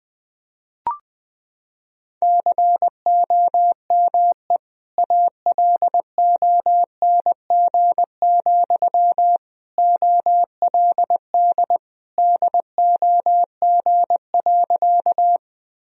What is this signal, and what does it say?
Morse code, heard as a keyed tone, about 20 wpm, 700 Hz. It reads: COME ALONG, OLD DOG.
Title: Sentences from Top 1000 Words - Review Part 2 - 20wpm